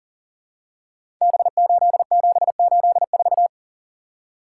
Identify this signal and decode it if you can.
Morse code, heard as a keyed tone, about 38 wpm, 700 Hz. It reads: B8784
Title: Code Groups - 5 Letter:Numbers - 40wpm